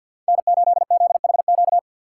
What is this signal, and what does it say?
Morse code, heard as a keyed tone, about 52 wpm, 700 Hz. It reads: N8ZHQ
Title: USA Call Signs - 3x Character Spacing - 50wpm